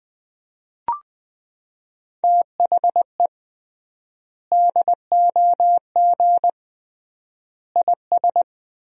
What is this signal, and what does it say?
Morse code, heard as a keyed tone, about 21 wpm, 700 Hz. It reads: THE DOG IS
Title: 3x Word-Spacing - Sentences from Top 400 Words - 20wpm